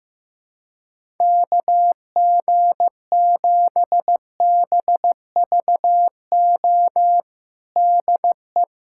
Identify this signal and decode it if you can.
Morse code, heard as a keyed tone, about 16 wpm, 700 Hz. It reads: KG7BVO DE
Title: USA Callsign DE Callsign - 6 Characters - 15wpm